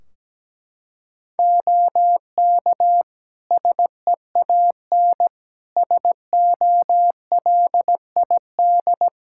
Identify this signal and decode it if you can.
Morse code, heard as a keyed tone, about 17 wpm, 700 Hz. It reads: OK SEAN SOLID
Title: Practice QSO - Exchange 3 - More Info - 17wpm